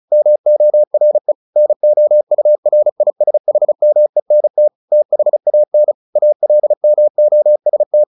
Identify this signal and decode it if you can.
Morse code, heard as a keyed tone, about 34 wpm, 600 Hz. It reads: MORE NOURISHMENT THAN ALMOST
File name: the old man and the sea 2 35wpm